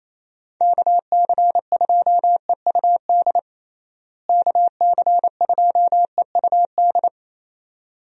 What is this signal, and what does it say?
Morse code, heard as a keyed tone, about 28 wpm, 700 Hz. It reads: KC2EUD KC2EUD
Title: USA Call Signs - 6 Characters - Repeated 2x - 28wpm